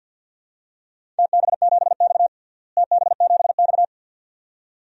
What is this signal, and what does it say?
Morse code, heard as a keyed tone, about 53 wpm, 700 Hz. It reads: TB7X TB7X
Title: DXCC-100 Call Signs - 4 Characters - Repeated 2x - 50wpm